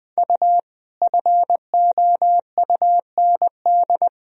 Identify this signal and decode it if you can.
Morse code, heard as a keyed tone, about 20 wpm, 700 Hz. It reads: U FOUND
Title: Sentences from Top 300 Words and Abbreviations 20wpm